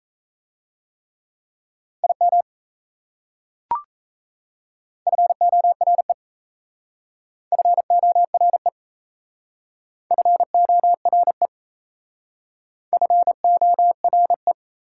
Morse code, heard as a keyed tone, about 38 wpm, 690 Hz. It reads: IM FORE FORE FORE FORE